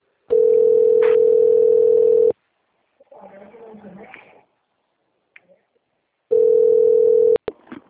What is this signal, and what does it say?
Ringback tone (North America), 440 + 480 Hz.